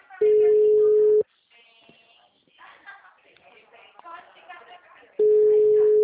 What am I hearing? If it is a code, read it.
Ringback tone (Europe, 425 Hz), with speech.
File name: out- -2009- - - .58172